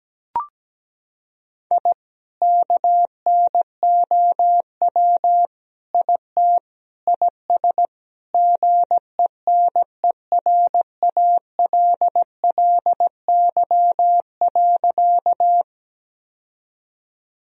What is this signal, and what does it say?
Morse code, heard as a keyed tone, about 17 wpm, 700 Hz. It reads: I KNOW IT IS GENERALLY.
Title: Sentences from Top 1000 Words - Review Part 4 - 17wpm